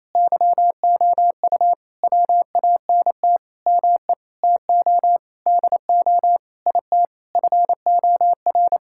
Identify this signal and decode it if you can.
Morse code, heard as a keyed tone, about 28 wpm, 700 Hz. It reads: YOU WANT ME TO DO IT FOR
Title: Sentences from Top 1000 Words - Review Part 6 - 28wpm